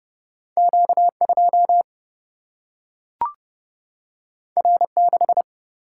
Morse code, heard as a keyed tone, about 30 wpm, 700 Hz. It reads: Q 2 R 6
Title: Single-Letter Single-Number 30wpm